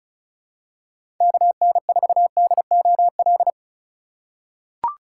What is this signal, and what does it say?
Morse code, heard as a keyed tone, about 36 wpm, 700 Hz. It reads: KN4DOL
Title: USA Call Signs - 2x Character Spacing - 35wpm